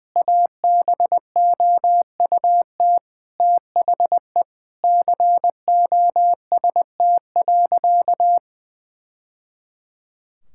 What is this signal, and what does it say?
Morse code, heard as a keyed tone, about 20 wpm, 700 Hz. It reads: ABOUT THE COST.
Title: Sentences from Top 1000 Words - Review Part 5 - 20wpm